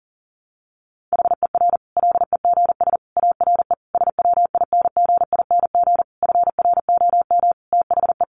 Morse code, heard as a keyed tone, about 40 wpm, 700 Hz. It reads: HER LEGS ARE SWINGING FROM THE
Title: Easy Phrases 40wpm